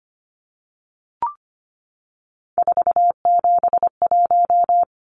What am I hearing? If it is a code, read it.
Morse code, heard as a keyed tone, about 25 wpm, 700 Hz. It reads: 471
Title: Numbers - 3-Digit 25wpm